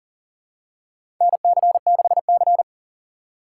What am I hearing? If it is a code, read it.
Morse code, heard as a keyed tone, about 40 wpm, 700 Hz. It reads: NC6C
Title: USA Call Signs - 6x Character Spacing - 40wpm